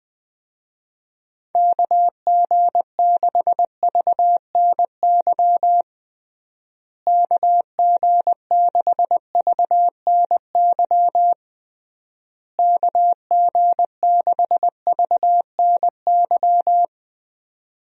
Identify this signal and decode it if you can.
Morse code, heard as a keyed tone, about 20 wpm, 700 Hz. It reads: KG6VNY KG6VNY KG6VNY
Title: USA Call Signs - 6 Characters - Repeated 3x - 20wpm